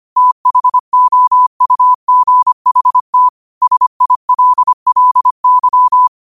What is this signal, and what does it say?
Morse code, heard as a keyed tone, about 25 wpm, 1000 Hz. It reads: THOUGHT SILLY
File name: qotdfiles / 25wpm